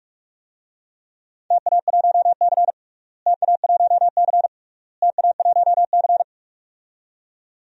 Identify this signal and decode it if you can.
Morse code, heard as a keyed tone, about 41 wpm, 700 Hz. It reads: TA1C TA1C TA1C
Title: DXCC-100 Call Signs - 4 Characters - Repeated 3x - 45wpm